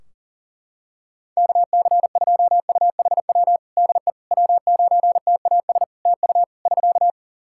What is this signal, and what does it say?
Morse code, heard as a keyed tone, about 40 wpm, 700 Hz. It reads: KC2UHW DE W9TAS TU <SK>
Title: Practice QSO - Exchange 4 - Ending - 40wpm